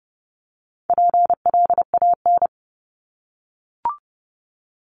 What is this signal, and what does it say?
Morse code, heard as a keyed tone, about 30 wpm, 700 Hz. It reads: PLAN
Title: ING Suffix - Top 100 Words with Base Words 30wpm